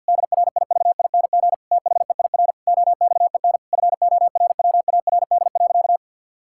Morse code, heard as a keyed tone, about 52 wpm, 700 Hz. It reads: DRIVING THEIR OXEN FORWARD.
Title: Pangram Challenge v2 - 50wpm